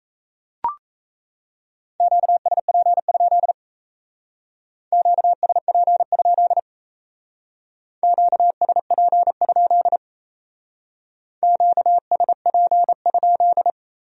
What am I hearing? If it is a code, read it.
Morse code, heard as a keyed tone, about 35 wpm, 700 Hz. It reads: QSP? QSP? QSP? QSP?
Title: Speed-Racing Q-Codes - 28wpm